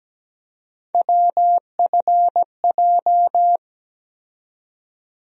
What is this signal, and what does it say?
Morse code, heard as a keyed tone, about 17 wpm, 700 Hz. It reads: WFJ
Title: USA Call Sign Postfixes - 3 Letters 17wpm